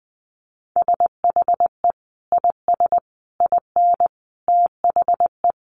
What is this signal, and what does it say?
Morse code, heard as a keyed tone, about 20 wpm, 700 Hz. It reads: SHE IS IN THE
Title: Sentences from Top 1200 Words 20wpm